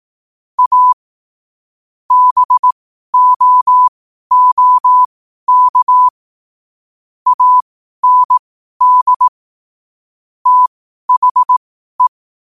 Morse code, heard as a keyed tone, about 18 wpm, spaced as for 12 wpm, 1000 Hz. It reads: A BOOK AND THE
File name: qotdfiles / 12wpm